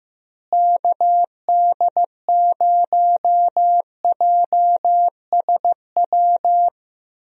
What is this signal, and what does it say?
Morse code, heard as a keyed tone, about 15 wpm, 700 Hz. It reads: KD0JSW
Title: USA Call Signs - 6 Characters - Repeated 3x - 15wpm